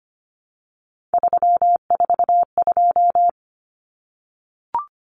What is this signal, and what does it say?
Morse code, heard as a keyed tone, about 25 wpm, 700 Hz. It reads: 342
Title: Numbers - 3-Digit 25wpm